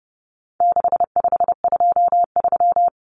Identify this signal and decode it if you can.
Morse code, heard as a keyed tone, about 30 wpm, 700 Hz. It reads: 6523
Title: Numbers - 4-Digit 30wpm